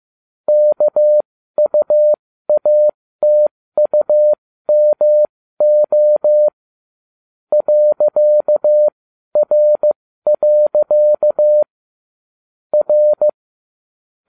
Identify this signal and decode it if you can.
Morse code, heard as a keyed tone, about 15 wpm, spaced as for 13 wpm, 600 Hz. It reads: KUATUMO .R. R
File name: cw / ubacw12 2